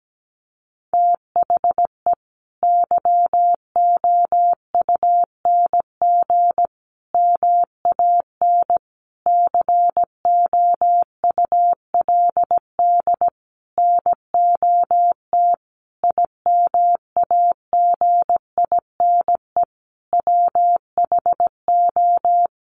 Morse code, heard as a keyed tone, about 17 wpm, 700 Hz. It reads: THE YOUNG MAN COULD NOT IMAGINE WHO
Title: Sentences from Top 1400 Words 17wpm